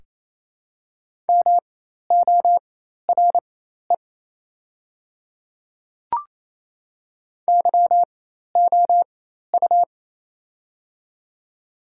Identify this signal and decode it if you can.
Morse code, heard as a keyed tone, about 28 wpm, spaced as for 13 wpm, 690 Hz. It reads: MORE YOU
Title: 4x Character Spacing - Top 100 Words - 28wpm